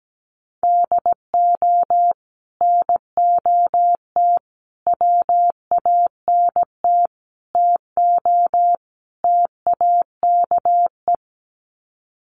Morse code, heard as a keyed tone, about 17 wpm, 700 Hz. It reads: DO NOT WANT TO TAKE
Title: Sets of 5 Words Encore2 17wpm